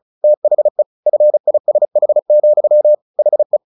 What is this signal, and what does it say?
Morse code, heard as a keyed tone, about 34 wpm, 600 Hz. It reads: THE FISH, HE
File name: the old man and the sea / the old man and the sea 1 35wpm